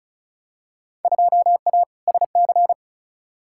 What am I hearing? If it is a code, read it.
Morse code, heard as a keyed tone, about 36 wpm, 700 Hz. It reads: 2A SC
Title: Field Day Exchanges - 35wpm